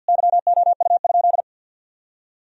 Morse code, heard as a keyed tone, about 50 wpm, 700 Hz. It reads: YOU?
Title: Sentences from Top 1000 Words - Review Part 4 - 50wpm